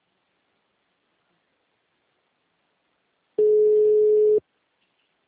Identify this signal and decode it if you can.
Ringback tone (Europe, 425 Hz).